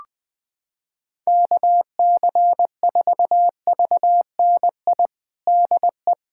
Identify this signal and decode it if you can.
Morse code, heard as a keyed tone, about 20 wpm, 700 Hz. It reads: KC4VNI DE
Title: USA Callsign DE Callsign 20wpm